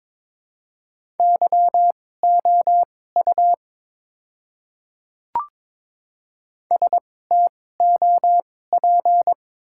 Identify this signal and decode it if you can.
Morse code, heard as a keyed tone, about 21 wpm, spaced as for 16 wpm, 700 Hz. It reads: YOU STOP
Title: 2x Character Spacing - Top 300 Words - 22wpm